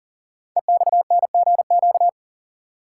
Morse code, heard as a keyed tone, about 41 wpm, 700 Hz. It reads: EXNGQ
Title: Code Groups - 5 Letters - 40wpm